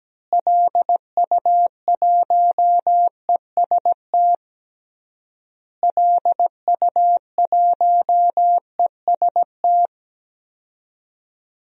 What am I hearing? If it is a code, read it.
Morse code, heard as a keyed tone, about 17 wpm, 700 Hz. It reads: LU1EST LU1EST
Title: DXCC-100 Call Signs - 6 Characters - Repeated 3x - 17wpm